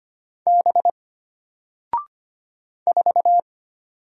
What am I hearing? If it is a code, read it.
Morse code, heard as a keyed tone, about 25 wpm, 700 Hz. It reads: B 4
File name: Single-Letter-Number-25wpm